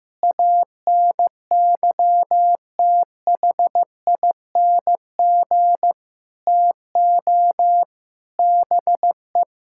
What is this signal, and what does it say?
Morse code, heard as a keyed tone, about 15 wpm, 700 Hz. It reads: ANYTHING TO BE
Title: Sentences from Top 1000 Words - Review Part 6 - 15wpm